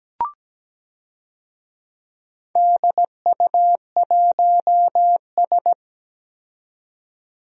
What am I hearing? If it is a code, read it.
Morse code, heard as a keyed tone, about 17 wpm, 700 Hz. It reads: DU1S
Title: DXCC-100 Call Signs - 4 Characters - 17wpm